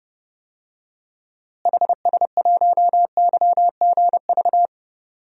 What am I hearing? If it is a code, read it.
Morse code, heard as a keyed tone, about 32 wpm, 700 Hz. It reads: HS1YGV
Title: DXCC-100 Call Signs - 6 Characters - Repeated 2x - 30wpm